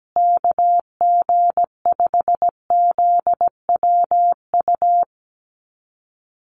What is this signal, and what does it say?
Morse code, heard as a keyed tone, about 17 wpm, 700 Hz. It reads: KG5ZWU
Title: USA Call Signs Repeated 3x 17wpm